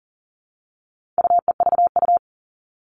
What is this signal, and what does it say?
Morse code, heard as a keyed tone, about 40 wpm, 700 Hz. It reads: UEVU
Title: All Cut Numbers - 4-Digit 40wpm